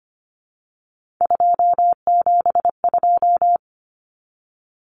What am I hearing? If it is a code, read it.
Morse code, heard as a keyed tone, about 25 wpm, 700 Hz. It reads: 272
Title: Numbers - 3-Digit 25wpm